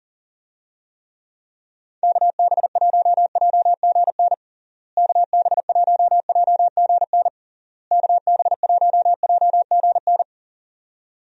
Morse code, heard as a keyed tone, about 40 wpm, 700 Hz. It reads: KB1JGN KB1JGN KB1JGN
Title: USA Call Signs - 6 Characters - Repeated 3x - 40wpm